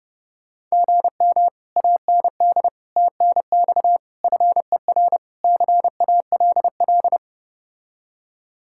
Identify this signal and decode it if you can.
Morse code, heard as a keyed tone, about 33 wpm, 700 Hz. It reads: GM AND TNX FER CALL